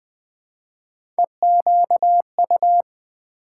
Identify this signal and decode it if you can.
Morse code, heard as a keyed tone, about 20 wpm, 700 Hz. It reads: EQU
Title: Build A Word - 1 Letter At A Time - 3 to 5-Letter Words - 20wpm